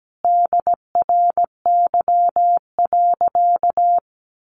Morse code, heard as a keyed tone, about 17 wpm, 700 Hz. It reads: DRY.
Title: Sentences from Top 1500 Words 17wpm